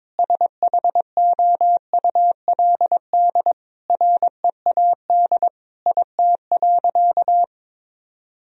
Morse code, heard as a keyed tone, about 22 wpm, 700 Hz. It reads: SHOULD READ IT.
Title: Sentences from Top 1000 Words - Review Part 1 - 22wpm